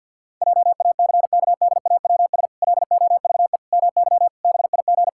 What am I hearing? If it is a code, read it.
Morse code, heard as a keyed tone, about 51 wpm, 700 Hz. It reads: JACKDAWS LOVE MY BIG
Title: Pangram Challenge v2 - 50wpm